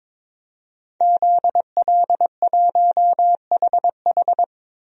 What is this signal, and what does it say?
Morse code, heard as a keyed tone, about 21 wpm, 700 Hz. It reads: ZL1HH
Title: DXCC-100 Call Signs - 5 Characters - 22wpm